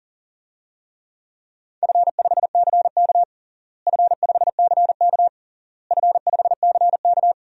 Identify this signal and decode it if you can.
Morse code, heard as a keyed tone, about 40 wpm, 700 Hz. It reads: F5CK F5CK F5CK
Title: DXCC-100 Call Signs - 4 Characters - Repeated 3x - 40wpm